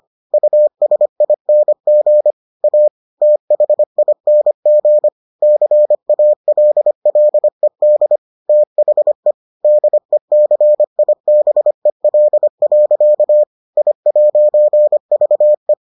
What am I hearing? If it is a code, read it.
Morse code, heard as a keyed tone, about 25 wpm, 600 Hz. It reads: USING A THING CALLED THE DECIBEL. I'VE